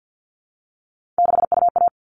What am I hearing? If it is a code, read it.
Morse code, heard as a keyed tone, about 50 wpm, 700 Hz. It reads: 6UA
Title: All Cut Numbers - 3-Digit 50wpm